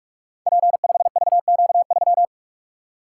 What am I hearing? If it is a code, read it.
Morse code, heard as a keyed tone, about 44 wpm, 700 Hz. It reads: P5VQ3
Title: Code Groups - 5 Letter:Numbers - 45wpm